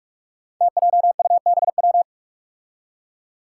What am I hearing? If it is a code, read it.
Morse code, heard as a keyed tone, about 41 wpm, 700 Hz. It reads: TJUBW
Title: Five Challenging Letter-Number Combinations 45wpm